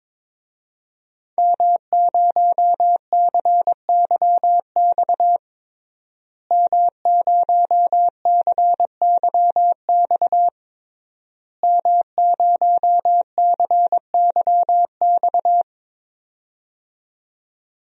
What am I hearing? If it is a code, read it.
Morse code, heard as a keyed tone, about 22 wpm, 700 Hz. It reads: M0CYX M0CYX M0CYX
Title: DXCC-100 Call Signs - 5 Characters - Repeated 3x - 22wpm